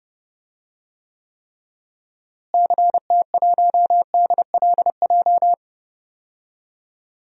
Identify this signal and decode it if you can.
Morse code, heard as a keyed tone, about 30 wpm, 700 Hz. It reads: CT1DLJ
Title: DXCC-100 Call Signs - 6 Characters - Repeated 3x - 30wpm